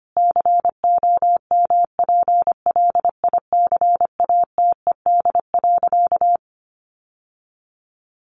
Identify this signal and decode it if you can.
Morse code, heard as a keyed tone, about 25 wpm, 700 Hz. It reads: COMPLICATED.
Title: Sentences from Top 1200 Words 25wpm